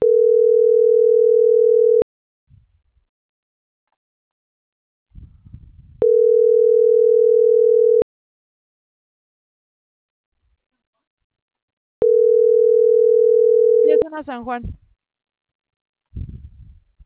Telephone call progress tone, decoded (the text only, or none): ringback tone